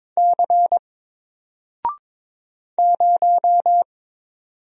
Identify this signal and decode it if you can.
Morse code, heard as a keyed tone, about 22 wpm, 700 Hz. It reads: C 0
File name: Single-Letter-Number-22wpm